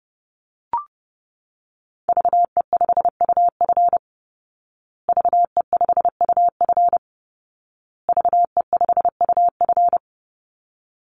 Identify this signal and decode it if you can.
Morse code, heard as a keyed tone, about 30 wpm, 700 Hz. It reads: VE5UF VE5UF VE5UF
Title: Call Signs Repeated 2x - 30wpm